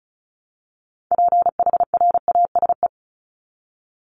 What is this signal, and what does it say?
Morse code, heard as a keyed tone, about 34 wpm, 700 Hz. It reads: PHRASE
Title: Top 1501-1750 Words 35wpm